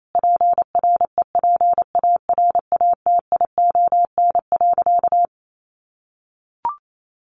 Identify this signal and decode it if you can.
Morse code, heard as a keyed tone, about 28 wpm, 700 Hz. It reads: PREPARATION.
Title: Sentences from Top 1400 Words 28wpm